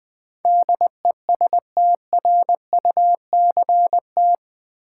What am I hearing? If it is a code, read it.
Morse code, heard as a keyed tone, about 20 wpm, 700 Hz. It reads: DESTRUCT
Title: Build A Word - 4 Letters At A Time - 11-Letter Words - 20wpm